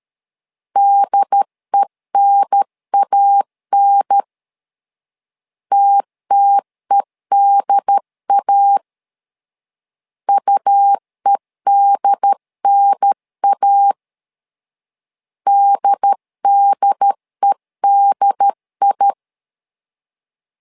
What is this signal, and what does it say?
Morse code, heard as a keyed tone, about 13 wpm, 800 Hz. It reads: DENAN TTEDA UEDNA DDEDI
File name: LEZIONI / 2 UHDN / 2 UHDN 12wpm 1456